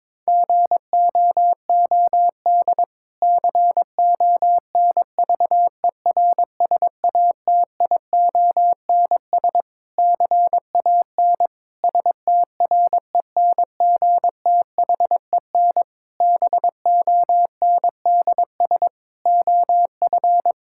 Morse code, heard as a keyed tone, about 22 wpm, 700 Hz. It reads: GOOD CONVERSATIONS CAN STRENGTHEN BONDS OF
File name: Difficult Sentences with Hints - 22wpm